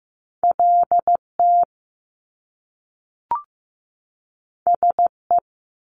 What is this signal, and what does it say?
Morse code, heard as a keyed tone, about 15 wpm, 700 Hz. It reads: LT SE
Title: Top Two Letter Combinations in Words 15wpm